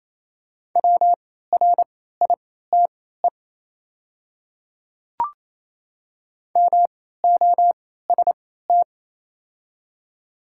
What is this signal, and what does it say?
Morse code, heard as a keyed tone, about 28 wpm, spaced as for 16 wpm, 700 Hz. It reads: WRITE MOST
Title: 3x Character Spacing - Top 100 Words - 28wpm